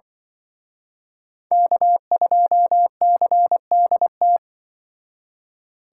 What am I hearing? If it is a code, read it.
Morse code, heard as a keyed tone, about 24 wpm, 700 Hz. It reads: K2CDT
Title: Speed Racing - USA 1x3 Call Signs 20wpm